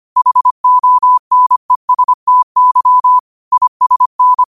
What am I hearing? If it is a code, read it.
Morse code, heard as a keyed tone, about 25 wpm, 1000 Hz. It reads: HONESTY ISN